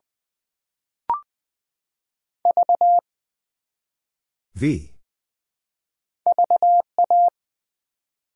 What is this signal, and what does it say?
Morse code, heard as a keyed tone, about 20 wpm, 700 Hz. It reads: V VA
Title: Build A Word - 1 Letter At A Time - 3-Letter Words - 20wpm